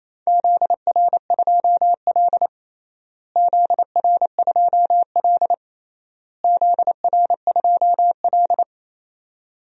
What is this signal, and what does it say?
Morse code, heard as a keyed tone, about 28 wpm, 700 Hz. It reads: ZR2L ZR2L ZR2L
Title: DXCC-100 Call Signs - 4 Characters - Repeated 3x - 28wpm